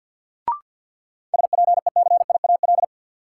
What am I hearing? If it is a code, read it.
Morse code, heard as a keyed tone, about 53 wpm, 700 Hz. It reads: SPECIAL
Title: Sentences from Top 1000 Words - Review Part 3 - New Word Drill - 50wpm